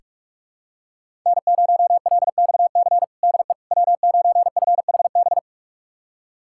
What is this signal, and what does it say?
Morse code, heard as a keyed tone, about 41 wpm, 700 Hz. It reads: N0LXC DE W9FHB
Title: USA Callsign DE Callsign - 5 Characters - 45wpm